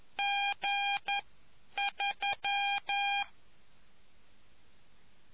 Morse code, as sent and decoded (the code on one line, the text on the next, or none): --. ...--
G3